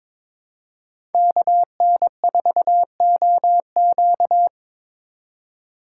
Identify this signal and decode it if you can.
Morse code, heard as a keyed tone, about 22 wpm, 700 Hz. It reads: KN4OQ
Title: USA Call Signs - 4x Character Spacing - 22wpm